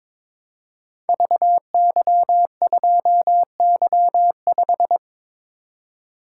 Morse code, heard as a keyed tone, about 22 wpm, 700 Hz. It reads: VY2Y5
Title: Code Groups - 5 Letter:Numbers - 22wpm